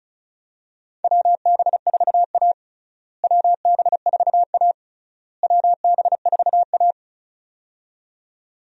Morse code, heard as a keyed tone, about 36 wpm, 700 Hz. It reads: WB4A WB4A WB4A
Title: USA Call Signs - 4 Characters - Repeated 3x - 35wpm